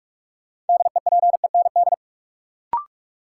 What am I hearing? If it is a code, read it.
Morse code, heard as a keyed tone, about 46 wpm, 700 Hz. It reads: DEPEND
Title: Sentences from Top 1300 Words - New Word Drill Alt - 45wpm